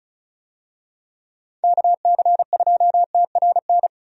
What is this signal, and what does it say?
Morse code, heard as a keyed tone, about 36 wpm, 700 Hz. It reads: KC2TRN
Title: USA Call Signs - 6 Characters - 35wpm